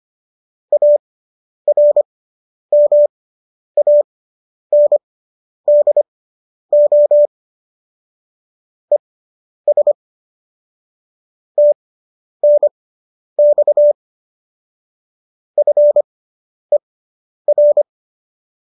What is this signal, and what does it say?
Morse code, heard as a keyed tone, about 25 wpm, spaced as for 10 wpm, 590 Hz. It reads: ARMANDO ES TNX FER